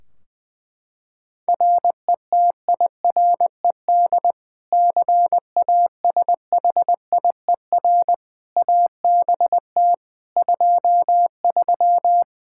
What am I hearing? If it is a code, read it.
Morse code, heard as a keyed tone, about 21 wpm, 700 Hz. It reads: RETIRED CASHIER ABT 23
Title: Practice QSO - Exchange 3 - More Info - 20wpm